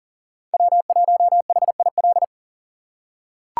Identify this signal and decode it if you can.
Morse code, heard as a keyed tone, about 38 wpm, 700 Hz. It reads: W1HIL
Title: USA Call Signs - 5 Characters - 6x Character Spacing - 40wpm